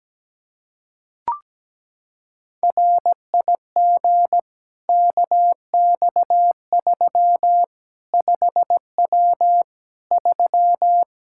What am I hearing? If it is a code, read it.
Morse code, heard as a keyed tone, about 17 wpm, 700 Hz. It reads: RIG KX3 5W 3
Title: K1USN SST QSO Practice 17wpm